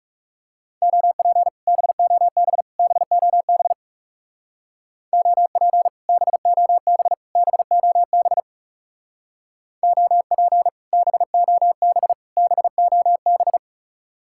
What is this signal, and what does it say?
Morse code, heard as a keyed tone, about 35 wpm, 700 Hz. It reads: OP BOB BOB OP BOB BOB OP BOB BOB
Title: Speed-Racing Common QSO Phrases - 30wpm